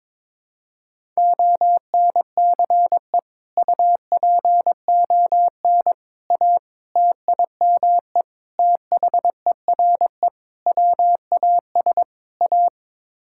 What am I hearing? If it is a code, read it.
Morse code, heard as a keyed tone, about 22 wpm, 700 Hz. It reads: ONCE UPON A TIME THERE WAS A
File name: practice / 22wpm / Sets of 7 Words - 22wpm